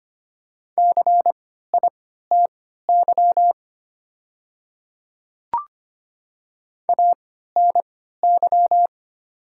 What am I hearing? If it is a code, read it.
Morse code, heard as a keyed tone, about 25 wpm, spaced as for 14 wpm, 700 Hz. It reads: CITY ANY
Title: 3x Character Spacing - Top 300 Words - 25wpm